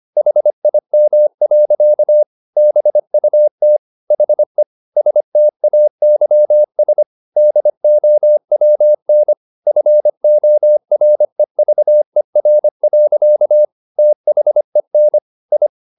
Morse code, heard as a keyed tone, about 25 wpm, 600 Hz. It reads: HIM. BUT HE STAYS DOWN FOREVER. THEN I
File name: the old man and the sea part 1 25wpm